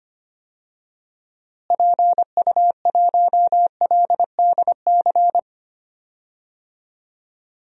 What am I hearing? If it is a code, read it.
Morse code, heard as a keyed tone, about 25 wpm, 700 Hz. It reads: PU1LDC
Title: DXCC-100 Call Signs - 6 Characters - Repeated 2x - 25wpm